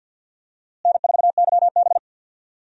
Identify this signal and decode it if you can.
Morse code, heard as a keyed tone, about 46 wpm, 700 Hz. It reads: N4YB
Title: USA Call Signs - 4 Characters - Repeated 3x - 50wpm